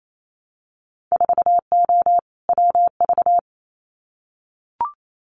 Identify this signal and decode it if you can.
Morse code, heard as a keyed tone, about 28 wpm, 700 Hz. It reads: 4O WV
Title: Winter Field Day Exchanges Encore 28wpm